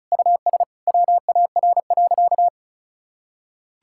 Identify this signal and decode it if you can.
Morse code, heard as a keyed tone, about 34 wpm, 700 Hz. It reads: US WAR.
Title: Sentences from Top 1000 Words - Review Part 6 - 35wpm